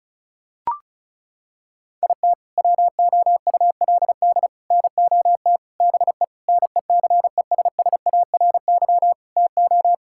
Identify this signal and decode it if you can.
Morse code, heard as a keyed tone, about 34 wpm, 700 Hz. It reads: IT WOULD NOT BE NECESSARY TO
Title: Sentences from Top 1600 Words 35wpm